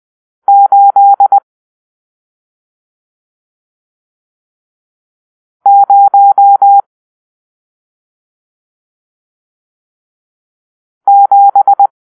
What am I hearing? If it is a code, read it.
Morse code, heard as a keyed tone, about 20 wpm, 800 Hz. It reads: 8 0 7
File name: arrl / Grp-7 Drill-1